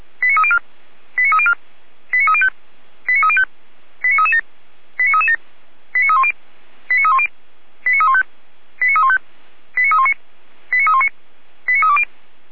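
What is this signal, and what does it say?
Five-tone selective call (ZVEI), heard as quick runs of five tones, tones of about 70 ms. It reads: 89304 89304 89305 89305 89337 89337 89310 89310 89315 89315 89319 89319 89320
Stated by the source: ZVEI five-tone call received on SDR.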